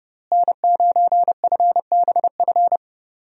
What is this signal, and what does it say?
Morse code, heard as a keyed tone, about 30 wpm, 700 Hz. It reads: N9FBF
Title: USA Call Signs - 5 Characters - Repeated 2x - 30wpm